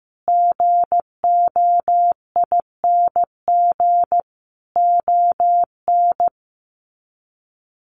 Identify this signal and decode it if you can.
Morse code, heard as a keyed tone, about 15 wpm, 700 Hz. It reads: GOING ON
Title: Sets of 5 Words Encore2 15wpm